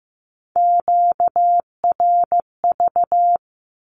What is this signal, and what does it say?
Morse code, heard as a keyed tone, about 15 wpm, 700 Hz. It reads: QRV
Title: Q-Codes with Spoken Meaning - 15wpm